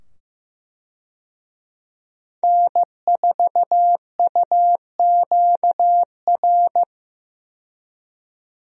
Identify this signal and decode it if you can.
Morse code, heard as a keyed tone, about 15 wpm, 700 Hz. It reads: N4UQR
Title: USA Call Signs - 5 Characters - 15wpm